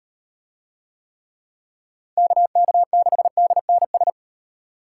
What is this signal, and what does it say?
Morse code, heard as a keyed tone, about 35 wpm, 700 Hz. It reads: KK6DNS